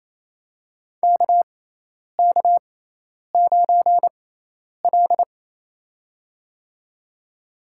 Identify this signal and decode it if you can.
Morse code, heard as a keyed tone, about 28 wpm, spaced as for 10 wpm, 700 Hz. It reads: KK9L
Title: USA Call Signs - 6x Character Spacing - 28wpm